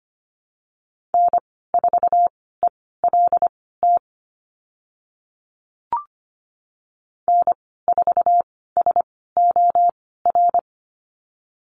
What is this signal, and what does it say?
Morse code, heard as a keyed tone, about 25 wpm, spaced as for 16 wpm, 700 Hz. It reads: N4ELT N4SOR
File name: Lesson 15g - Number 5 - Call Signs 25wpm